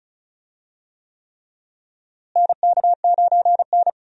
Morse code, heard as a keyed tone, about 36 wpm, 700 Hz. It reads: NK9N
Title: USA Call Signs - 4 Characters - Repeated 3x - 35wpm